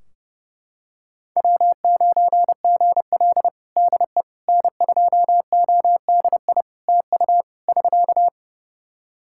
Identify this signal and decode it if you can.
Morse code, heard as a keyed tone, about 30 wpm, 700 Hz. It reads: W9GL DE N2ODI TU <SK>
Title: Practice QSO - Exchange 4 - Ending - 30wpm